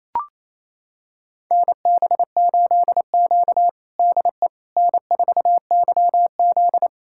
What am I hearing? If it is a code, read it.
Morse code, heard as a keyed tone, about 28 wpm, 700 Hz. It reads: NB8Q DE N4YZ
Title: USA Callsign DE Callsign - 4 Characters - 28wpm